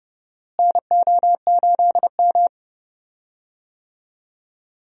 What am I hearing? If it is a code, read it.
Morse code, heard as a keyed tone, about 31 wpm, 700 Hz. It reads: NO8M
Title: USA Call Signs - 4 Characters - Repeated 2x - 30wpm